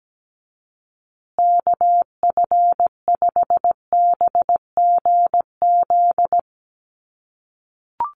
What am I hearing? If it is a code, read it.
Morse code, heard as a keyed tone, about 17 wpm, 700 Hz. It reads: KF5BGZ
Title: USA Call Signs Repeated 3x 17wpm